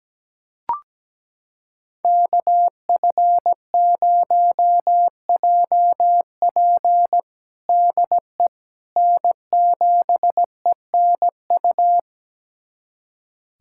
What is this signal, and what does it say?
Morse code, heard as a keyed tone, about 17 wpm, 700 Hz. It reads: KF0JP DE N7ENU
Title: USA Callsign DE Callsign - 5 Characters - 17wpm